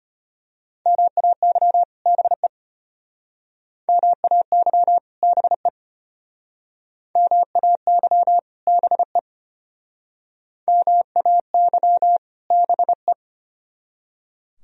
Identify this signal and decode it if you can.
Morse code, heard as a keyed tone, about 31 wpm, 700 Hz. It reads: MAY BE MAY BE MAY BE MAY BE
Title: Speed-Racing - Sets of 2 Words 25wpm